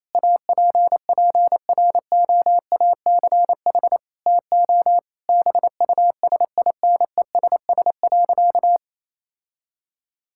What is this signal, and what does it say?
Morse code, heard as a keyed tone, about 27 wpm, 700 Hz. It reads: APPROACH TO BUSINESS.
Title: Sentences from Top 1000 Words - Review Part 6 - 28wpm